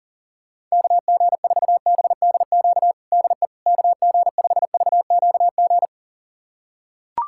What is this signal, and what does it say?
Morse code, heard as a keyed tone, about 40 wpm, 700 Hz. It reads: KG4BDQ DE KG5VQG
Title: USA Callsign DE Callsign 40wpm